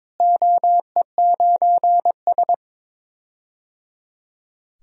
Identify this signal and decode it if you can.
Morse code, heard as a keyed tone, about 22 wpm, 700 Hz. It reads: OE9S
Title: DXCC-100 Call Signs - 4 Characters - Repeated 3x - 22wpm